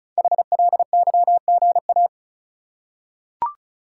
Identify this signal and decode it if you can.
Morse code, heard as a keyed tone, about 36 wpm, 700 Hz. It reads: HLYGA
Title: Code Groups - 5 Letters - 35wpm